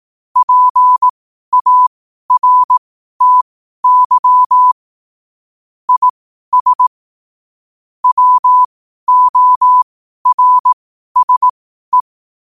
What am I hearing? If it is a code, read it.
Morse code, heard as a keyed tone, about 18 wpm, spaced as for 12 wpm, 1000 Hz. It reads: PARTY IS WORSE